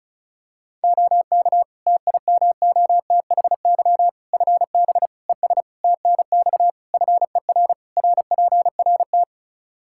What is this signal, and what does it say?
Morse code, heard as a keyed tone, about 34 wpm, 700 Hz. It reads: OK TIMOTHY FB ES TNX FER RPRT
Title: Practice QSO - Exchange 2 - Info - 35wpm